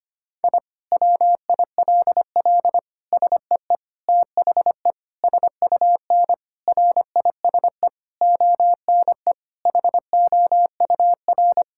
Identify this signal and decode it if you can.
Morse code, heard as a keyed tone, about 25 wpm, 700 Hz. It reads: I WILL SEE THE SUN RISE ONE HOUR
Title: Sentences from Top 1000 Words - Review Part 6 - 25wpm